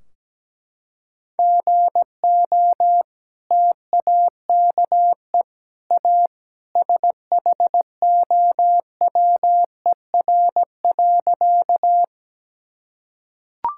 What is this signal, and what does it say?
Morse code, heard as a keyed tone, about 17 wpm, 700 Hz. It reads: GO TAKE A SHOWER.
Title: Sentences from Top 1600 Words 17wpm